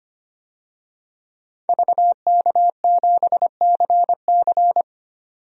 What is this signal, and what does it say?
Morse code, heard as a keyed tone, about 25 wpm, 700 Hz. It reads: VK7CC
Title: DXCC-100 Call Signs - 5 Characters - 25wpm